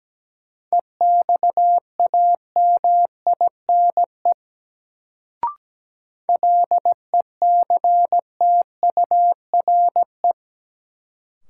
Morse code, heard as a keyed tone, about 17 wpm, 700 Hz. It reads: EXAMINE LECTURE
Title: Sentences from Top 1600 Words - New Word Drill - 17wpm